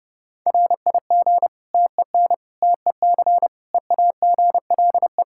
Morse code, heard as a keyed tone, about 32 wpm, 700 Hz. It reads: RIG TEN TEC EAGLE
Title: Practice QSO - Exchange 2 - Info - 30wpm